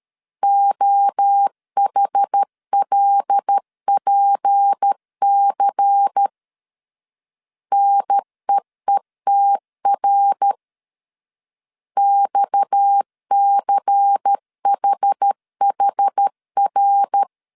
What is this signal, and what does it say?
Morse code, heard as a keyed tone, about 13 wpm, 800 Hz. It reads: OHLPC NEETR XCHHR
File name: CW / DET / 1 6 1500C S A 12wpm 3440